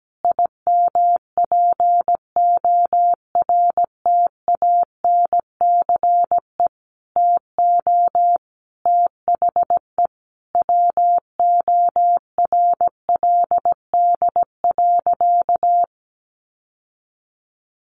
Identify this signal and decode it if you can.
Morse code, heard as a keyed tone, about 17 wpm, 700 Hz. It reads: IMPORTANCE TO THE WORLD.
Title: Sentences from Top 1200 Words 17wpm